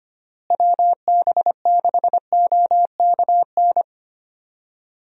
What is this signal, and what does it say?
Morse code, heard as a keyed tone, about 25 wpm, 700 Hz. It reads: WB6OKN
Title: USA Call Signs - 6 Characters - Repeated 3x - 25wpm